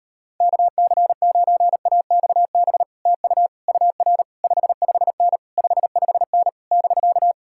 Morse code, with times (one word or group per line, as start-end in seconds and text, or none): 0.4-2.8 KC9AXB
3.1-3.5 TU
3.7-4.2 UR
4.4-5.4 55N
5.6-6.5 55N
6.7-7.3 <BK>